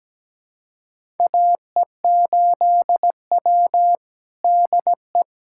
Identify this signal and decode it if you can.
Morse code, heard as a keyed tone, about 17 wpm, 700 Hz. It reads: AE8W DE
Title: USA Callsign DE Callsign - 4 Characters - 17wpm